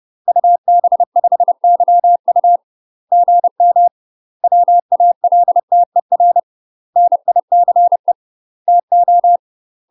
Morse code, heard as a keyed tone, about 30 wpm, 700 Hz. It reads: UB5YU GM WALTER NICE TO
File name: UK MASTER QSOs TRAINING / 30WPM / UK MASTER QSOs 0010